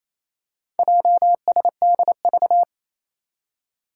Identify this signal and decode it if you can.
Morse code, heard as a keyed tone, about 28 wpm, 700 Hz. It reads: JSDV
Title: Four Challenging Letter-Number Combinations 28wpm